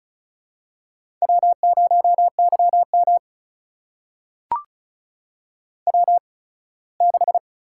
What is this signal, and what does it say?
Morse code, heard as a keyed tone, about 36 wpm, spaced as for 21 wpm, 700 Hz. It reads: W0YM W6
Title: USA Call Signs - 4 Characters - 8x Character Spacing - 35wpm